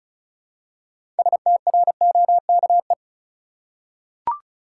Morse code, heard as a keyed tone, about 36 wpm, 700 Hz. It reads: STROKE
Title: Six Letter Words - 35wpm